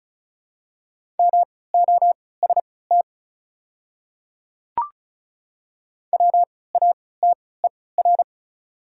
Morse code, heard as a keyed tone, about 36 wpm, spaced as for 20 wpm, 700 Hz. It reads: MOST WATER
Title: 3x Character Spacing - Top 500 Words - 35wpm